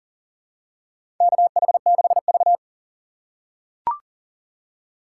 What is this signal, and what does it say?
Morse code, heard as a keyed tone, about 37 wpm, 700 Hz. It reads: KH6V
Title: USA Call Signs - 4 Characters - 3x Character Spacing - 40wpm